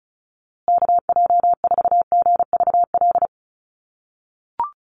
Morse code, heard as a keyed tone, about 34 wpm, 700 Hz. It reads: KJ4GVL
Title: USA Call Signs Repeated 3x 35wpm